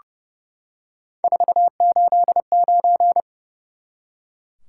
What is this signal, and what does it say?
Morse code, heard as a keyed tone, about 31 wpm, 700 Hz. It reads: 489
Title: Top 180 QSO Elements 30wpm